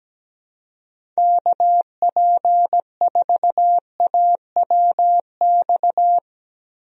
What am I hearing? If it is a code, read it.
Morse code, heard as a keyed tone, about 17 wpm, 700 Hz. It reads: KP4AWX
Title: DXCC-100 Call Signs - 6 Characters - Repeated 3x - 17wpm